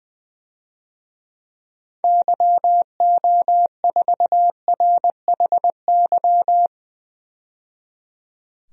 Morse code, heard as a keyed tone, about 20 wpm, 700 Hz. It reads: YO4RHY
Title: DXCC-100 Call Signs - 6 Characters - 20wpm